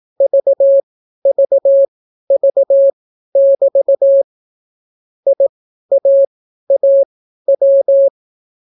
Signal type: Morse code